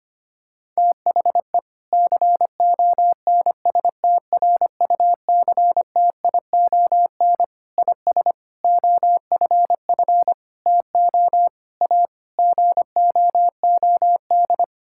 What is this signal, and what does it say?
Morse code, heard as a keyed tone, about 25 wpm, 700 Hz. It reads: THE CONSTRUCTION IS OFF TO A GOOD
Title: Sentences from Top 1600 Words 25wpm